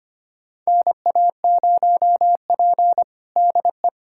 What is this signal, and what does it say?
Morse code, heard as a keyed tone, about 25 wpm, 700 Hz. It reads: NA0P DE
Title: USA Callsign DE Callsign - 4 Characters - 25wpm